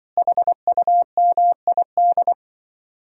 Morse code, heard as a keyed tone, about 23 wpm, 700 Hz. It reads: HUMID
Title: Speed-Racing Common QSO Phrases - 20wpm